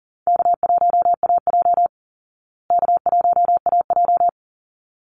Morse code, heard as a keyed tone, about 40 wpm, 700 Hz. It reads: K1AJ K1AJ
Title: Call Signs Repeated 3x - 40wpm